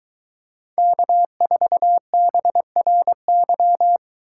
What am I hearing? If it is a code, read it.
Morse code, heard as a keyed tone, about 23 wpm, 700 Hz. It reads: K4BRY